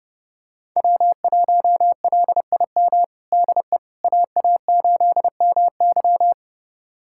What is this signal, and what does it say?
Morse code, heard as a keyed tone, about 30 wpm, 700 Hz. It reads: W1LIM DE AA8MY
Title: USA Callsign DE Callsign - 5 Characters - 30wpm